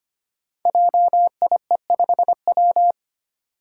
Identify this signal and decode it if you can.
Morse code, heard as a keyed tone, about 25 wpm, 700 Hz. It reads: JIE5W
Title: Code Groups - 5 Letter:Numbers - 25wpm